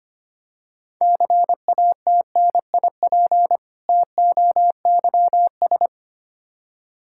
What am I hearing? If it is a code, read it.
Morse code, heard as a keyed tone, about 25 wpm, 700 Hz. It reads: CATNIP TOYS
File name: practice / 25wpm / Difficult Words with Hints - 25wpm